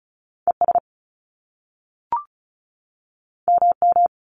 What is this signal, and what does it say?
Morse code, heard as a keyed tone, about 35 wpm, 700 Hz. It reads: E S M M
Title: Top Two Letter Combinations in Words 35wpm